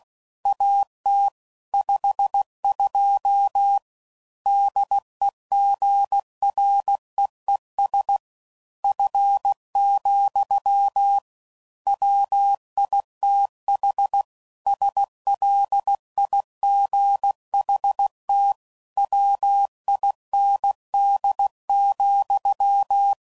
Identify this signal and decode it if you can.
Morse code, heard as a keyed tone, about 16 wpm, 800 Hz. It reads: AT 52 DEGREES F, WITH SLIGHT WIND,
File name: CW Code Groups @15WPM / Weather Sets / Weather 4 17